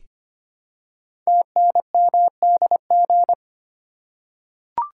Morse code, heard as a keyed tone, about 25 wpm, 700 Hz. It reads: TNMDG
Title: Code Groups - 5 Letters - 25wpm